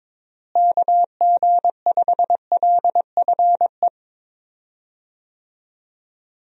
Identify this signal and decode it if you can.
Morse code, heard as a keyed tone, about 22 wpm, 700 Hz. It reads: KG5LFE
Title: USA Call Signs - 6 Characters - Repeated 3x - 22wpm